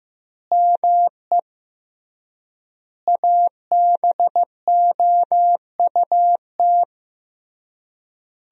Morse code, heard as a keyed tone, about 15 wpm, 700 Hz. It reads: ME ABOUT
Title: 3x Word-Spacing - Sentences from Top 200 Words - 15wpm